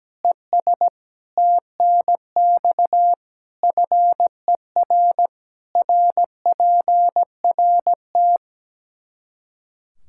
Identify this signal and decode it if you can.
Morse code, heard as a keyed tone, about 17 wpm, 700 Hz. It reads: ES TNX FER RPRT
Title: Practice QSO - Exchange 2 - Info - 17wpm